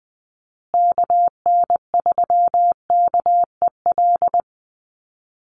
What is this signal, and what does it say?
Morse code, heard as a keyed tone, about 20 wpm, 700 Hz. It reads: KN3KEL
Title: USA Call Signs Repeated 3x 20wpm